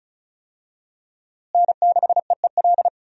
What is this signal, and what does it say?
Morse code, heard as a keyed tone, about 34 wpm, 700 Hz. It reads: N6EEL
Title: USA Call Signs - 5 Characters - Repeated 2x - 35wpm